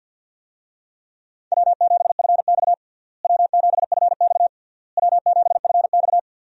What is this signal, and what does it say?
Morse code, heard as a keyed tone, about 49 wpm, 700 Hz. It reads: W7FX W7FX W7FX
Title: USA Call Signs - 4 Characters - Repeated 3x - 50wpm